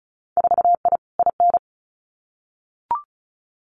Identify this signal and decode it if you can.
Morse code, heard as a keyed tone, about 36 wpm, 700 Hz. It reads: 4I IN
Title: Winter Field Day Exchanges Encore 35wpm